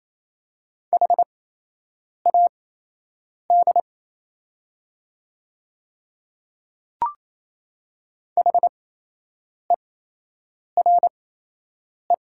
Morse code, heard as a keyed tone, about 28 wpm, spaced as for 8 wpm, 700 Hz. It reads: HAD HERE
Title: 8x Character Spacing - Top 200 Words - 28wpm